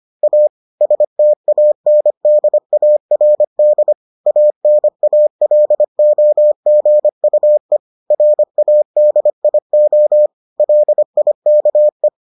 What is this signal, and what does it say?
Morse code, heard as a keyed tone, about 25 wpm, 600 Hz. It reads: A STANDARD ANALOGUE RADIO LIKE